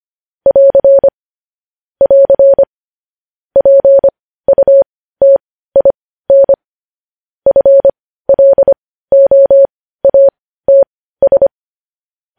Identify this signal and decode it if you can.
Morse code, heard as a keyed tone, about 25 wpm, spaced as for 15 wpm, 550 Hz. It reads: <AR> <AR> PUTIN FLOATS
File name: podcast / News headlines-1x-25x15